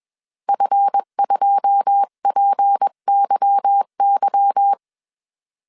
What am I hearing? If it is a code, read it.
Morse code, heard as a keyed tone, about 21 wpm, 800 Hz. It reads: F2PYY
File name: CW / DET / 1 9 1500C S B 20wpm 2313